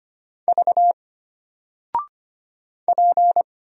Morse code, heard as a keyed tone, about 25 wpm, 700 Hz. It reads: V P
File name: Single-Letter-Number-25wpm